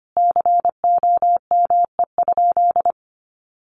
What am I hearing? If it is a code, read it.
Morse code, heard as a keyed tone, about 25 wpm, 700 Hz. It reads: COME?
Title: Sentences from Top 200 Words 25wpm